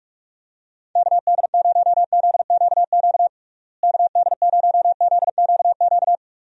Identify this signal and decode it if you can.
Morse code, heard as a keyed tone, about 44 wpm, 700 Hz. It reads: KD0ZQQ KD0ZQQ
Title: USA Call Signs - 6 Characters - Repeated 2x - 45wpm